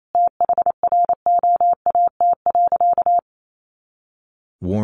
Morse code, heard as a keyed tone, about 28 wpm, 700 Hz. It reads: THROAT.
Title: A Sound Of Thunder 28wpm